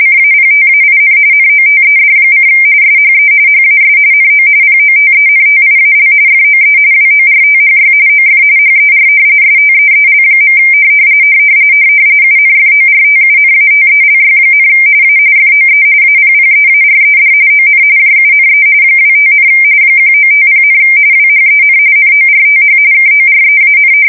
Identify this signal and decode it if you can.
RTTY, about 45.45 baud. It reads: CAPTAIN NOTED: SHE HAS NO POTENTIAL FOE WORTHY OF HER STEEL. UNFORTUNATELY, A FOE APPEARED ON JUNE 25, 1950, WHEN NORTH KOREAN
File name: fsk170 / battleshipnj